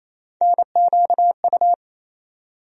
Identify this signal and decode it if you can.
Morse code, heard as a keyed tone, about 28 wpm, 700 Hz. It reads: NQU
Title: USA Call Sign Postfixes - 3 Letters 28wpm